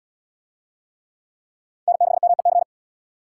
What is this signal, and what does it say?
Morse code, heard as a keyed tone, about 73 wpm, 700 Hz. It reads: N6MEC